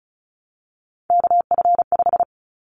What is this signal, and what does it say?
Morse code, heard as a keyed tone, about 34 wpm, 700 Hz. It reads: KF5
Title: Callsign Prefixes for Top 25 DXCC - 35wpm